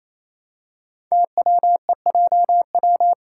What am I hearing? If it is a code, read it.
Morse code, heard as a keyed tone, about 27 wpm, 700 Hz. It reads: TWEJW
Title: Code Groups - 5 Letters - 28wpm